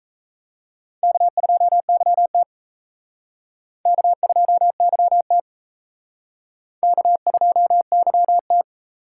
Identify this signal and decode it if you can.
Morse code, heard as a keyed tone, about 34 wpm, 700 Hz. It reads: K2YT K2YT K2YT
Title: Speed Racing - USA 1x2 Call Signs 28wpm